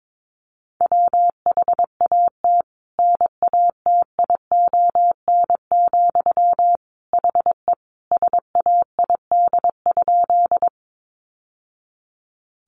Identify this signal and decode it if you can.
Morse code, heard as a keyed tone, about 22 wpm, 700 Hz. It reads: WHAT NATION, HE SAID?
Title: Sentences from Top 900 Words 22wpm